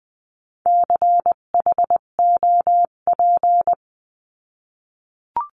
Morse code, heard as a keyed tone, about 20 wpm, 700 Hz. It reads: CHOP
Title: 258 Lesson 42c - Words 16-20wpm